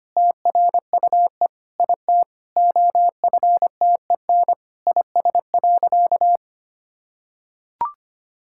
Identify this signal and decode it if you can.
Morse code, heard as a keyed tone, about 25 wpm, 700 Hz. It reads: TRUE IT OFTEN IS.
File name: practice / 25wpm / Sentences with 9-Word Phrases - 25wpm